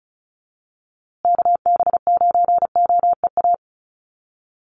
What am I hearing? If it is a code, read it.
Morse code, heard as a keyed tone, about 34 wpm, 700 Hz. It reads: KB9OEA
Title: USA Call Signs Repeated 2x 35wpm